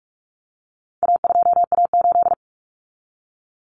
Morse code, heard as a keyed tone, about 44 wpm, 700 Hz. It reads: A2A8
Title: Common Cut Numbers - 4-Digit 45wpm